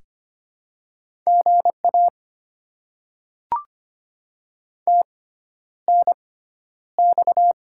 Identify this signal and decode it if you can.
Morse code, heard as a keyed tone, about 25 wpm, spaced as for 11 wpm, 700 Hz. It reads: GA TNX
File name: Top 20 QSO Elements - 6x Character Spacing - 25wpm